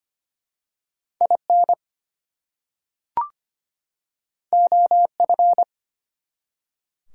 Morse code, heard as a keyed tone, about 25 wpm, 700 Hz. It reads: IN OF
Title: Two Letter Words - 25wpm